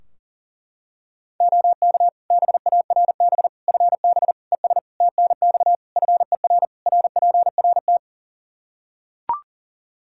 Morse code, heard as a keyed tone, about 40 wpm, 700 Hz. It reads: OK BARB FB ES TNX FER RPRT
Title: Practice QSO - Exchange 2 - Info - 40wpm